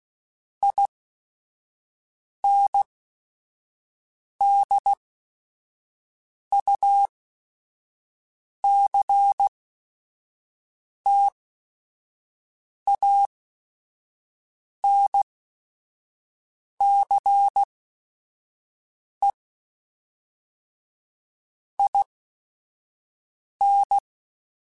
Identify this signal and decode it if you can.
Morse code, heard as a keyed tone, about 16 wpm, spaced as for 5 wpm, 800 Hz. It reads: INDUCTANCE IN
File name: Technical Subjects / Technical Subjects-mp3s / XL @5